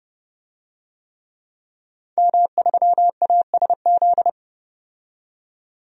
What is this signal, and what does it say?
Morse code, heard as a keyed tone, about 29 wpm, 690 Hz. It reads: M3ASZ
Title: DXCC-100 Call Signs - 5 Characters - 30wpm